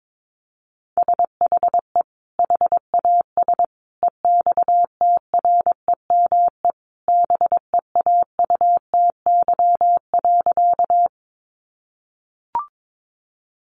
Morse code, heard as a keyed tone, about 22 wpm, 700 Hz. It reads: SHE HAS EXTREME BEAUTY.
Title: Sentences from Top 1300 Words 22wpm